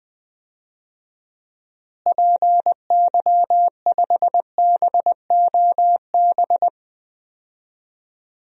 Morse code, heard as a keyed tone, about 20 wpm, 700 Hz. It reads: PY5BOB
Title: DXCC-100 Call Signs - 6 Characters - Repeated 2x - 20wpm